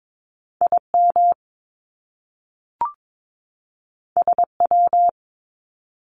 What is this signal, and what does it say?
Morse code, heard as a keyed tone, about 22 wpm, 700 Hz. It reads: IM SW
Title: Top Two Letter Combinations in Words 22wpm